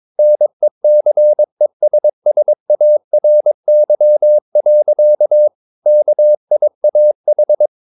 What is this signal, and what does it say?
Morse code, heard as a keyed tone, about 22 wpm, 600 Hz. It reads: NECESSARY. KIAH
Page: Romanen Never af Ken Follet på engelsk.